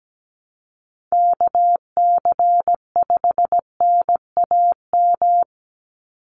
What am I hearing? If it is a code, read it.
Morse code, heard as a keyed tone, about 17 wpm, 700 Hz. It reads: KC5NAM
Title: USA Call Signs 17wpm